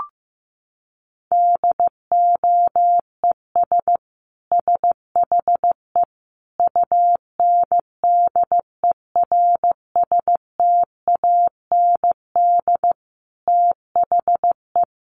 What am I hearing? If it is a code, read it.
Morse code, heard as a keyed tone, about 15 wpm, 700 Hz. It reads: DOES SHE UNDERSTAND THE
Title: Sentences from Top 1100 Words 15wpm